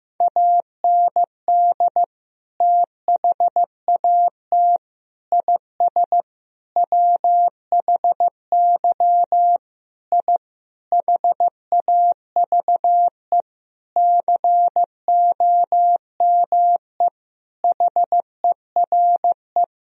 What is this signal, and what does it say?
Morse code, heard as a keyed tone, about 15 wpm, 700 Hz. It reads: AND THAT IS WHY I HAVE COME HERE
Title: Sentences from Top 1000 Words - Review Part 3 - 15wpm